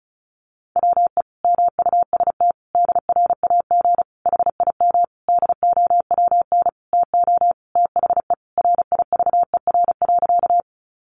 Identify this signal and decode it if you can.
Morse code, heard as a keyed tone, about 34 wpm, 700 Hz. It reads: WE MUST DRAG HIM DOWN TO THE RIVER.
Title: Sentences from Top 1500 Words 35wpm